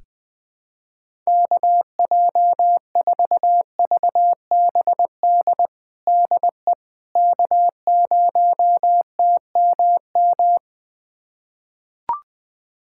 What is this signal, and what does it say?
Morse code, heard as a keyed tone, about 21 wpm, 700 Hz. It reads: KJ4VBD DE K0TMM
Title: USA Callsign DE Callsign 20wpm